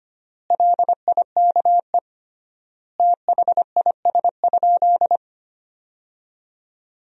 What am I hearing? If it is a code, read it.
Morse code, heard as a keyed tone, about 25 wpm, 700 Hz. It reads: LIKE THIS?
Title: 3x Word-Spacing - Sentences from Top 200 Words - 25wpm